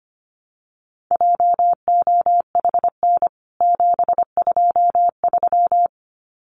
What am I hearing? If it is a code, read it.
Morse code, heard as a keyed tone, about 25 wpm, 700 Hz. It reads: JOHN 723
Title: CWT Exchange 25wpm